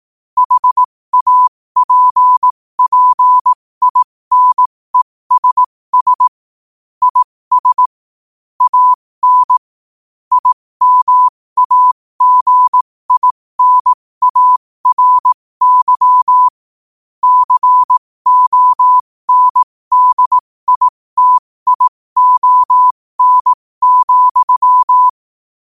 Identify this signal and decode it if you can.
Morse code, heard as a keyed tone, about 18 wpm, spaced as for 15 wpm, 1000 Hz. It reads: HAPPINESS IS AN IMAGINARY CONDITION,
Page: New quotes every day in morse code at 15 Words per minute.